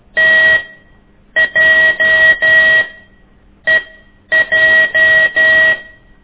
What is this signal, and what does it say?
Morse code, heard as a keyed tone, about 12 wpm, spaced as for 8 wpm, 1900 Hz, with noise. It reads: TJEJ
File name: ik1weg2 / CORSI CW